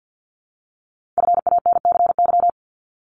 Morse code, heard as a keyed tone, about 50 wpm, 700 Hz. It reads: FANCY
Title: Multi-letter Phonemes Repeated 3x 50wpm